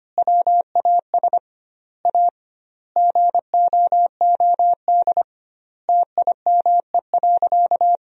Morse code, heard as a keyed tone, about 25 wpm, 700 Hz. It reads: WAS A GOOD TIME.
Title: 2x Word-Spacing - Sentences from Top 200 Words - 25wpm